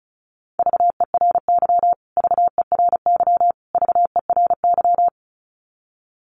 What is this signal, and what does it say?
Morse code, heard as a keyed tone, about 34 wpm, 700 Hz. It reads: VERY VERY VERY
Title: Top 300 Words Repeated 35wpm